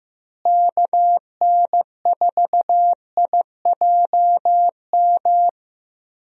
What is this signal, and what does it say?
Morse code, heard as a keyed tone, about 15 wpm, 700 Hz. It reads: KN4IJM
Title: USA Call Signs - 6 Characters - Repeated 2x - 15wpm